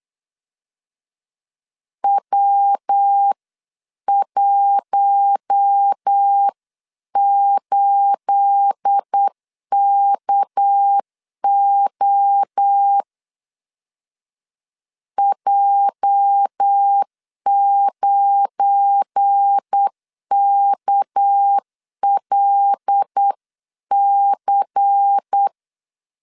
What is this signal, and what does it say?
Morse code, heard as a keyed tone, about 8 wpm, 800 Hz. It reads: W18KO J9KLC
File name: CW / DET / 1 9 1500C S C 8wpm 5653